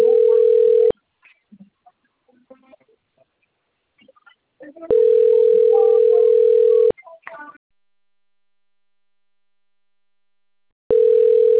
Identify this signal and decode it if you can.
Ringback tone (North America), 440 + 480 Hz.
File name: out- - - - - .87178